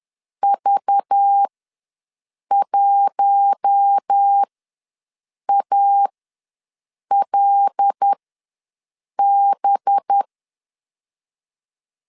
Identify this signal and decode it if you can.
Morse code, heard as a keyed tone, about 10 wpm, spaced as for 6 wpm, 800 Hz. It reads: V1ALB
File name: LEZIONI / 5 LBVC / 5 LBV ws 10wpm 3645